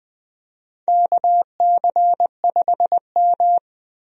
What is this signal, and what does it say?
Morse code, heard as a keyed tone, about 20 wpm, 700 Hz. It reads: KC5M
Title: USA Call Signs - 4 Characters - 20wpm